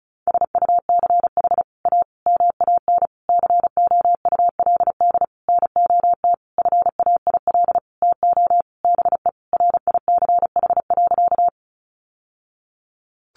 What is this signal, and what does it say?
Morse code, heard as a keyed tone, about 34 wpm, 700 Hz. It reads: SUCH A MAN COULD NOT FAIL TO BE RICH.
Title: Sentences from Top 1100 Words 35wpm